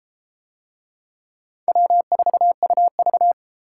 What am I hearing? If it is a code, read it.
Morse code, heard as a keyed tone, about 33 wpm, 700 Hz. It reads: W4UV